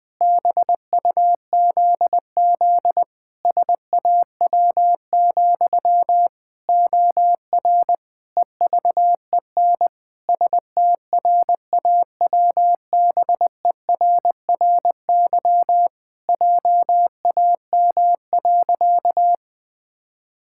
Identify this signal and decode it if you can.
Morse code, heard as a keyed tone, about 20 wpm, 700 Hz. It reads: BUZZ SAW, OR EVEN STRAWBERRY JAM.
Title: Pangram Challenge v2 - 20wpm